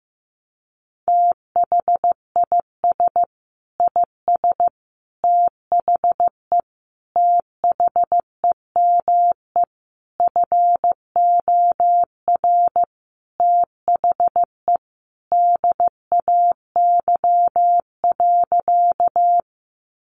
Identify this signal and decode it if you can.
Morse code, heard as a keyed tone, about 15 wpm, 700 Hz. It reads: THIS IS THE THEME FOR THE DAY.
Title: Sentences from Top 1100 Words 15wpm